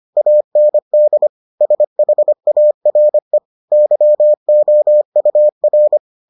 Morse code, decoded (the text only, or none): AND SHARE YOUR